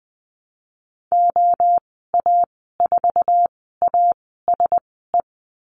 Morse code, heard as a keyed tone, about 20 wpm, spaced as for 15 wpm, 700 Hz. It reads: OA4ASE
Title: 048 Lesson 9g - Number 4 - Call Signs 20wpm